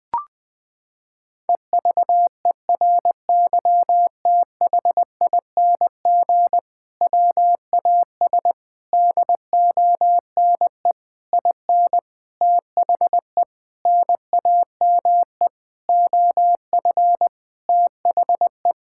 Morse code, heard as a keyed tone, about 20 wpm, 700 Hz. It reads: EVERYTHING WAS DONE IN THE NAME OF THE
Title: Sentences from Top 1600 Words 20wpm